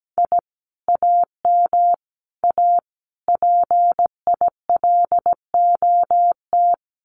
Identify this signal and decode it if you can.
Morse code, heard as a keyed tone, about 17 wpm, 700 Hz. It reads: I AM A PILOT
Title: Common QSO Phrases Encore 17wpm